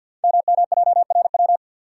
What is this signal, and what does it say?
Morse code, heard as a keyed tone, about 50 wpm, 690 Hz. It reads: MMJRW
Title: Code Groups - 5 Letters - 50wpm